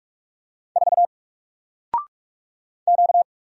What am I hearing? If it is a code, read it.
Morse code, heard as a keyed tone, about 43 wpm, 700 Hz. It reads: 4 Q
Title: Single Character (Letter:Number:Punctuation:Prosign) Alt 45wpm